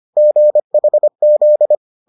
Morse code, heard as a keyed tone, about 25 wpm, 600 Hz. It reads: GHZ